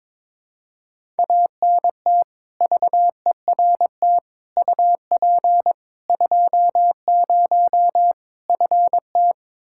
Morse code, heard as a keyed tone, about 22 wpm, 700 Hz. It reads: ANT VERT UP 20 FT
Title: K1USN SST QSO Practice 22wpm